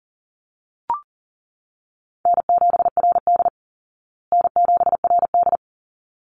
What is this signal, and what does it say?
Morse code, heard as a keyed tone, about 38 wpm, 700 Hz. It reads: N7RD N7RD
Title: Call Signs Repeated 3x - 40wpm